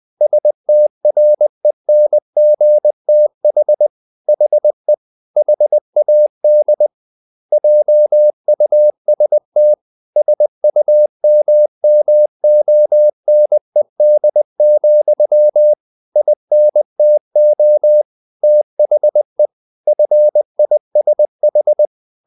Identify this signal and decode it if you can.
Morse code, heard as a keyed tone, about 20 wpm, 600 Hz. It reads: STRENGTH HE HAD JUST SUMMONED, INTO THE FISH
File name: the old man and the sea part 2 20wpm